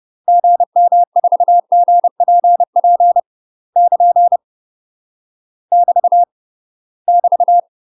Morse code, heard as a keyed tone, about 30 wpm, 700 Hz. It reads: GM4GPP <KN> <BT> <BT>
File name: UK MASTER QSOs TRAINING / 30WPM / UK MASTER QSOs 0033